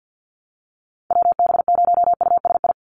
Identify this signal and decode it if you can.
Morse code, heard as a keyed tone, about 50 wpm, 700 Hz. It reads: WB0USI
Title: NA SOTA Activator Callsigns 50wpm